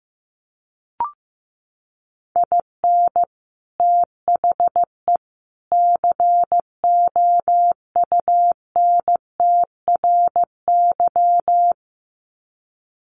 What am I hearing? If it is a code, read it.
Morse code, heard as a keyed tone, about 15 wpm, 700 Hz. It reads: IN THE COUNTRY
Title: Sets of 3 Words 15wpm